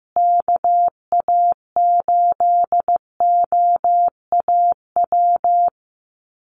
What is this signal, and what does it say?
Morse code, heard as a keyed tone, about 15 wpm, 700 Hz. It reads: KA8OAW
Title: USA Call Signs Repeated 3x 15wpm